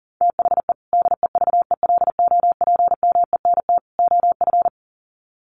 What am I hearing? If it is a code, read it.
Morse code, heard as a keyed tone, about 40 wpm, 700 Hz. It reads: THE DEVELOPMENT OF
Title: Sets of 3 Words 40wpm